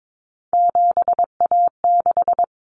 Morse code, heard as a keyed tone, about 22 wpm, 700 Hz. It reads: 7A6
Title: Common Cut Numbers - 3-Digit 22wpm